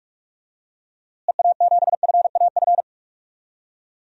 Morse code, heard as a keyed tone, about 46 wpm, 700 Hz. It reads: EA7FAF
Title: DXCC-100 Call Signs - 6 Characters - Repeated 3x - 45wpm